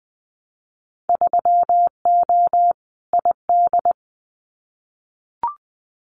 Morse code, heard as a keyed tone, about 20 wpm, 700 Hz. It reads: 3O ID
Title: Winter Field Day Exchanges 20wpm